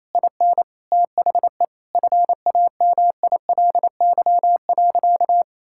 Morse code, heard as a keyed tone, about 28 wpm, 700 Hz. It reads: IN THE FAMILY.
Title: Sentences from Top 1000 Words - Review Part 6 - 28wpm